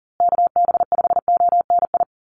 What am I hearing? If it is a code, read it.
Morse code, heard as a keyed tone, about 40 wpm, 700 Hz. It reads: KB5ONI
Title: USA Call Signs Repeated 3x 40wpm